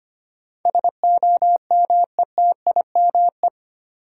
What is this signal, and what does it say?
Morse code, heard as a keyed tone, about 25 wpm, 700 Hz. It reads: SOMETIME
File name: Eight Letter Words - 25wpm